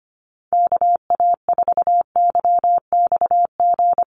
Morse code, heard as a keyed tone, about 25 wpm, 700 Hz. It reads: KA4YXG
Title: USA Call Signs Repeated 2x 25wpm